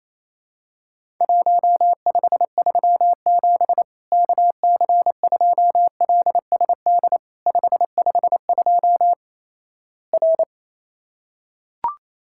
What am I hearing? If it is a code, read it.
Morse code, heard as a keyed tone, about 28 wpm, 700 Hz. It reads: 1537 KC2LSD 552 R
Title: Worked All Europe DX Contest - QTC Exchange - 28wpm